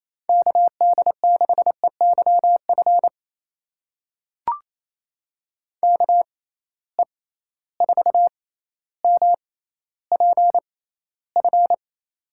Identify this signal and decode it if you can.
Morse code, heard as a keyed tone, about 28 wpm, spaced as for 14 wpm, 700 Hz. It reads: KD6EYF KE4MPF
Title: USA Call Signs - 6 Characters - 6x Character Spacing - 28wpm